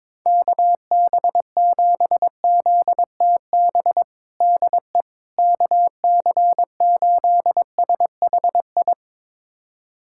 Morse code, heard as a keyed tone, about 22 wpm, 700 Hz. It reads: KB7ZTB DE KC8SHI
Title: USA Callsign DE Callsign - 6 Characters - 22wpm